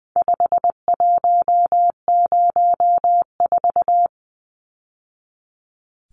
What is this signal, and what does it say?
Morse code, heard as a keyed tone, about 20 wpm, 700 Hz. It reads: 5104
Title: Numbers - 4-Digit 20wpm